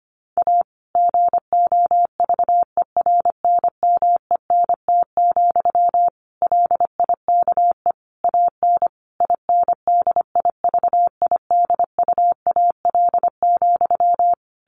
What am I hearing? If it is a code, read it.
Morse code, heard as a keyed tone, about 25 wpm, 700 Hz. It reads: A GOVERNMENT, LIKE AN INDIVIDUAL,
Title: Sentences from Top 1400 Words 25wpm